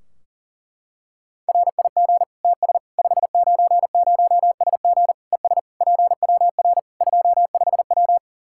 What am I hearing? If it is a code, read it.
Morse code, heard as a keyed tone, about 40 wpm, 700 Hz. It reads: RIG TS 590SG ES PWR 25W
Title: K1USN SST QSO Practice 40wpm